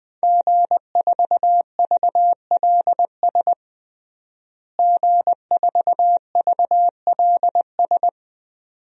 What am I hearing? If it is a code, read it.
Morse code, heard as a keyed tone, about 20 wpm, 700 Hz. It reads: G4VLS G4VLS
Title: DXCC-100 Call Signs - 5 Characters - Repeated 2x - 20wpm